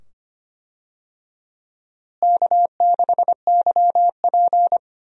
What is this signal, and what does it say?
Morse code, heard as a keyed tone, about 25 wpm, 700 Hz. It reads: K6YP
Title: USA Call Signs - 4 Characters - Repeated 2x - 25wpm